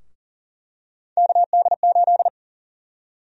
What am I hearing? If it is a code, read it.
Morse code, heard as a keyed tone, about 40 wpm, 700 Hz. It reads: KD8
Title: USA Call Sign Prefixes - 40wpm